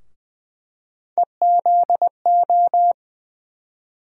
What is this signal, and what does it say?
Morse code, heard as a keyed tone, about 20 wpm, 700 Hz. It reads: EZO
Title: USA Call Sign Postfixes - 3 Letters 20wpm